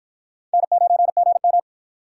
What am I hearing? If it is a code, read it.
Morse code, heard as a keyed tone, about 53 wpm, 700 Hz. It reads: N9GM